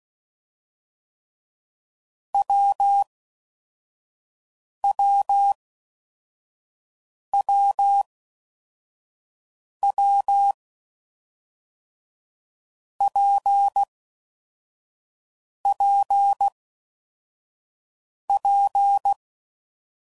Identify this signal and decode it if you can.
Morse code, heard as a keyed tone, about 16 wpm, 800 Hz. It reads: W W W W P P P
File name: Combinations / Difficult Combos / WPJ WPJ1